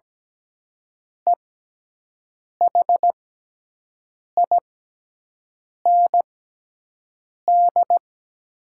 Morse code, heard as a keyed tone, about 17 wpm, spaced as for 6 wpm, 700 Hz. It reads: EHIND
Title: 6x Character Spacing - Top 500 Words - 17wpm